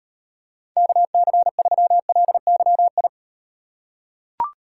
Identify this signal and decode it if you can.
Morse code, heard as a keyed tone, about 38 wpm, 700 Hz. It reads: KC3LYI